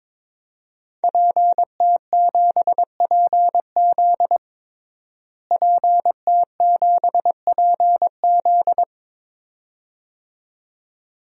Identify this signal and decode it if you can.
Morse code, heard as a keyed tone, about 21 wpm, 700 Hz. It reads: PT7PZ PT7PZ
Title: DXCC-100 Call Signs - 5 Characters - Repeated 2x - 22wpm